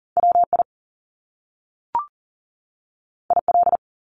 Morse code, heard as a keyed tone, about 39 wpm, 700 Hz. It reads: WI IL
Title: US States 40wpm